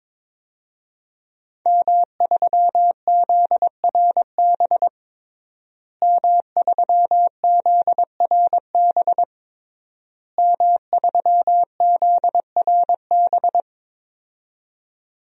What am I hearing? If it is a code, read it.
Morse code, heard as a keyed tone, about 21 wpm, 700 Hz. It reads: M3ZRB M3ZRB M3ZRB
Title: DXCC-100 Call Signs - 5 Characters - Repeated 3x - 22wpm